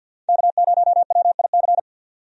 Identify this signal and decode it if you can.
Morse code, heard as a keyed tone, about 50 wpm, 700 Hz. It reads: K0WIC
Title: USA Call Signs - 5 Characters - 50wpm